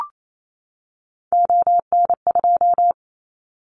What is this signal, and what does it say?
Morse code, heard as a keyed tone, about 28 wpm, 700 Hz. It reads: ON2
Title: Callsign Prefixes for Top 50 DXCC - 28wpm